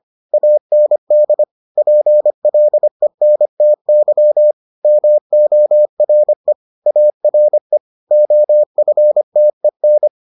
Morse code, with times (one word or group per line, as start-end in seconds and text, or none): 0.3-1.4 AND
1.8-4.5 PLENTY
4.8-6.5 MORE
6.9-7.8 ARE
8.1-10.1 OFTEN